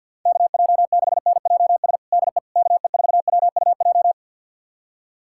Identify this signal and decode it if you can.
Morse code, heard as a keyed tone, about 48 wpm, 700 Hz. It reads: KJ6NJS DE KE4WUJ
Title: USA Callsign DE Callsign 50wpm